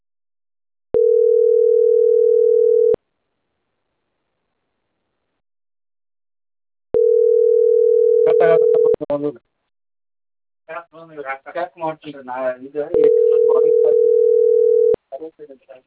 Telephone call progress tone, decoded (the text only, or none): ringback tone